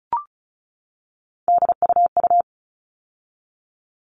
Morse code, heard as a keyed tone, about 34 wpm, 700 Hz. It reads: DUU
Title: All Cut Numbers - 3-Digit 35wpm